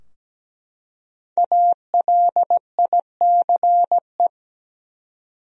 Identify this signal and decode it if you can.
Morse code, heard as a keyed tone, about 17 wpm, 700 Hz. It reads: ALICE
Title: USA Common Names - Max 5 Letters - 17wpm